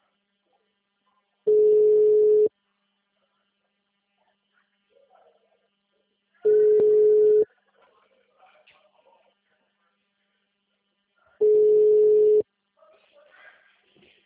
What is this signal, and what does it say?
Ringback tone (Europe, 425 Hz), with speech.